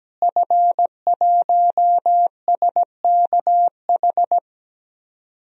Morse code, heard as a keyed tone, about 17 wpm, 700 Hz. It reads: F1SKH
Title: DXCC-100 Call Signs - 5 Characters - Repeated 3x - 17wpm